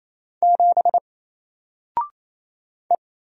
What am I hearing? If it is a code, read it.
Morse code, heard as a keyed tone, about 28 wpm, 700 Hz. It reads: 7 E
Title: Single Character (Letter:Number:Punctuation:Prosign) 28wpm